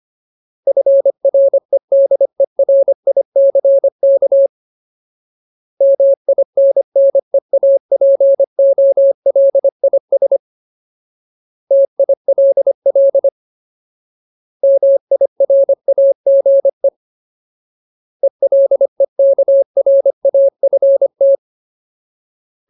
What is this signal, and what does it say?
Morse code, heard as a keyed tone, about 25 wpm, 550 Hz. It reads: FREDERICK MINNEAPOLIS TILL MIRAGE ELEKRAFT
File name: Ultimate Word List / 25WPM Ultimate